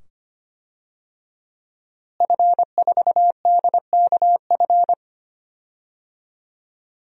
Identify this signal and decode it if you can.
Morse code, heard as a keyed tone, about 25 wpm, 700 Hz. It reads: F4DKF
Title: DXCC-100 Call Signs - 5 Characters - Repeated 2x - 25wpm